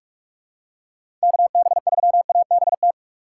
Morse code, heard as a keyed tone, about 46 wpm, 700 Hz. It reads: KB3ABT
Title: USA Call Signs - 6 Characters - 3x Character Spacing - 45wpm